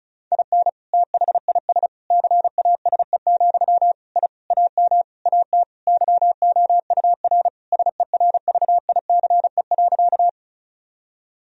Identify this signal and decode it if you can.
Morse code, heard as a keyed tone, about 36 wpm, 700 Hz. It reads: IN THIS CASE, I AM AT YOUR SERVICE.
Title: Sentences from Top 1000 Words - Review Part 4 - 35wpm